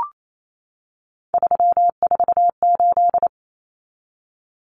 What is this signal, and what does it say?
Morse code, heard as a keyed tone, about 28 wpm, 700 Hz. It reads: 348
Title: Common Cut Numbers - 3-Digit 28wpm